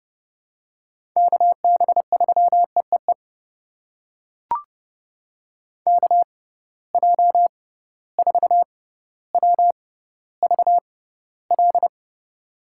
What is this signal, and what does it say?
Morse code, heard as a keyed tone, about 29 wpm, spaced as for 15 wpm, 700 Hz. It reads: KB3EEE KJ4WVL
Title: USA Call Signs - 6x Character Spacing - 30wpm